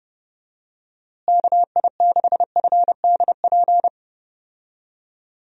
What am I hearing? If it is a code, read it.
Morse code, heard as a keyed tone, about 30 wpm, 700 Hz. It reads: KI6FDP
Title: USA Call Signs - 6 Characters - Repeated 3x - 30wpm